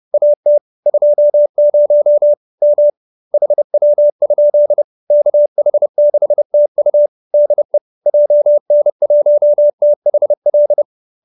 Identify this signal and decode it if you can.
Morse code, heard as a keyed tone, about 30 wpm, 590 Hz. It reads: AT 20 M HW? KH6TU DE JN1THL
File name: QSO309 30